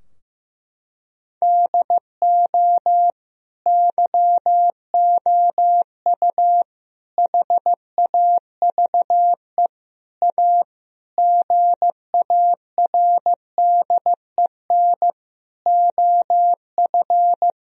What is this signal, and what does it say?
Morse code, heard as a keyed tone, about 16 wpm, 700 Hz. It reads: DO YOU HAVE A GARDEN OF
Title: Sentences from Top 1000 Words - Review Part 4 - 15wpm